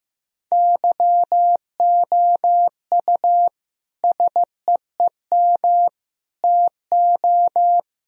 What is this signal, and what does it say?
Morse code, heard as a keyed tone, about 15 wpm, 700 Hz. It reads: YOU SEEM TO
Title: Sentences from Top 1000 Words - Review Part 3 - 15wpm